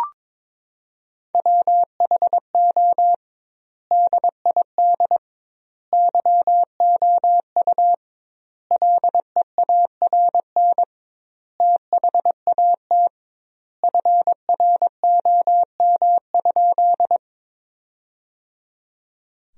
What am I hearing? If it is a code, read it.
Morse code, heard as a keyed tone, about 21 wpm, 700 Hz. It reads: WHO DID YOU LEARN THAT FROM?
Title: 2x Word-Spacing - Sentences from Top 300 Words - 22wpm